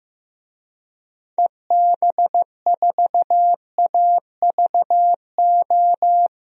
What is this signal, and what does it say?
Morse code, heard as a keyed tone, about 15 wpm, 700 Hz. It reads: EB4AVO
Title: DXCC-100 Call Signs - 6 Characters - Repeated 3x - 15wpm